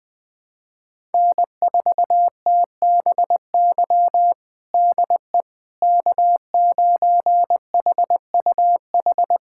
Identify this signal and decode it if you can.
Morse code, heard as a keyed tone, about 20 wpm, 700 Hz. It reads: N4TBY DE K9HUH
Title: USA Callsign DE Callsign - 5 Characters - 20wpm